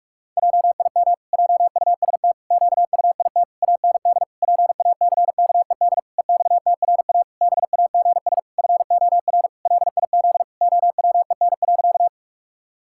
Morse code, heard as a keyed tone, about 46 wpm, 700 Hz. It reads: JIM JUST QUIT AND PACKED EXTRA BAGS FOR LIZ OWEN.
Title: Pangram Challenge v2 - 45wpm